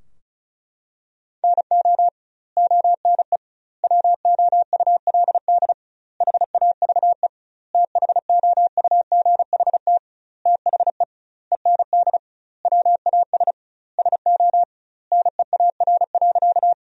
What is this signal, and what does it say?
Morse code, heard as a keyed tone, about 36 wpm, 700 Hz. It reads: NO ONE WOULD HAVE THOUGHT THE END WAS SO NEAR.
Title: 2x Word-Spacing - Sentences from Top 300 Words - 35wpm